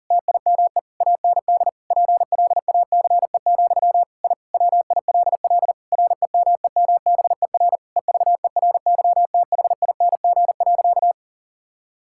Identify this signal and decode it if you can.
Morse code, heard as a keyed tone, about 40 wpm, 700 Hz. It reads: TIME AND PLACE, I WILL REMEMBER EVERYTHING.
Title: Sentences from Top 1000 Words - Review Part 5 - 40wpm